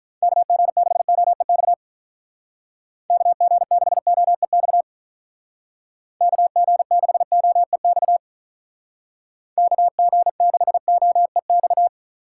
Morse code, heard as a keyed tone, about 43 wpm, 700 Hz. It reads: KG6OEX KG6OEX KG6OEX KG6OEX
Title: Speed Racing - USA 2x3 Call Signs 35wpm